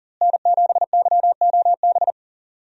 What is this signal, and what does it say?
Morse code, heard as a keyed tone, about 40 wpm, 700 Hz. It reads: N7YOB
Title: USA Call Signs - 5 Characters - Repeated 2x - 40wpm